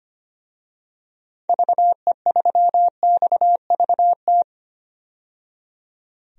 Morse code, heard as a keyed tone, about 25 wpm, 700 Hz. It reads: VE3XVT
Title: DXCC-100 Call Signs - 6 Characters - 25wpm